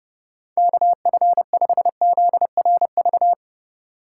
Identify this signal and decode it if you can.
Morse code, heard as a keyed tone, about 30 wpm, 700 Hz. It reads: KF5ZRV
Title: USA Call Signs - 6 Characters - 4x Character Spacing - 30wpm